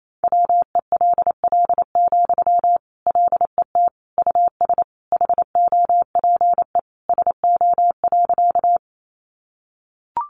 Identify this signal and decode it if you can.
Morse code, heard as a keyed tone, about 28 wpm, 700 Hz. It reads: WELL, LET US HOPE SO.
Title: Sentences from Top 1000 Words 28wpm